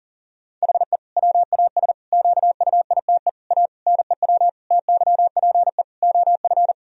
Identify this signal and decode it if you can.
Morse code, heard as a keyed tone, about 40 wpm, 700 Hz. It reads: HE WAS QUITE A NEW TYPE OF
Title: Sentences from Top 1000 Words - Review Part 3 - 40wpm